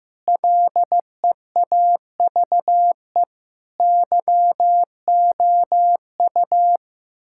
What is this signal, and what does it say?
Morse code, heard as a keyed tone, about 15 wpm, 700 Hz. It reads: LEAVE YOU
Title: Sentences from Top 1600 Words 15wpm